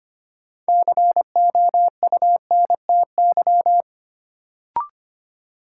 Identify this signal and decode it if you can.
Morse code, heard as a keyed tone, about 25 wpm, 700 Hz. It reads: COUNTY
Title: Sentences from Top 1200 Words - New Word Drill - 25wpm